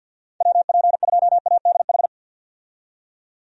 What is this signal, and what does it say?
Morse code, heard as a keyed tone, about 51 wpm, 700 Hz. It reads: WP2ADH
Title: USA Call Signs - 6 Characters - 50wpm